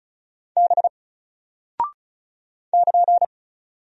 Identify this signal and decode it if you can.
Morse code, heard as a keyed tone, about 34 wpm, 700 Hz. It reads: B <KN>
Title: Single Character (Letter:Number:Punctuation:Prosign) Alt 35wpm